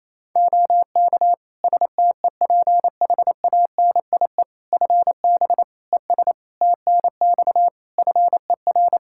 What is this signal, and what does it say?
Morse code, heard as a keyed tone, about 28 wpm, 700 Hz. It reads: OK STEPHANIE FB ES TNX FER
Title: Practice QSO - Exchange 2 - Info - 28wpm